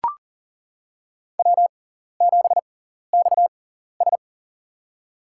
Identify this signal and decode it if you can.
Morse code, heard as a keyed tone, about 40 wpm, spaced as for 14 wpm, 700 Hz. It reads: W7XS
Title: USA Call Signs - 4 Characters - 6x Character Spacing - 40wpm